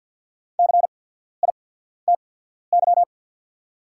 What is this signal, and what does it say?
Morse code, heard as a keyed tone, about 48 wpm, spaced as for 14 wpm, 690 Hz. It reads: CITY
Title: 8x Character Spacing - Top 500 Words - 50wpm